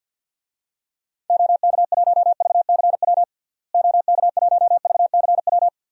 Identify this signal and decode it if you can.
Morse code, heard as a keyed tone, about 47 wpm, 700 Hz. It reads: OK1VCW OK1VCW
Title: DXCC-100 Call Signs - 6 Characters - Repeated 3x - 50wpm